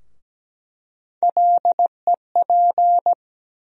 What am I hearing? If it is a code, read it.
Morse code, heard as a keyed tone, about 17 wpm, 700 Hz. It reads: LEP
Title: USA Call Sign Postfixes - 3 Letters 17wpm